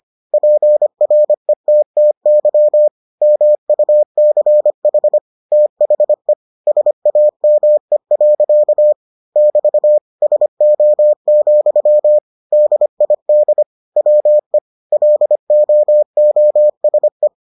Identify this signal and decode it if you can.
Morse code, heard as a keyed tone, about 25 wpm, 600 Hz. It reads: PRETTY MUCH THE SAME. <BT> SO, DID WE LOOSE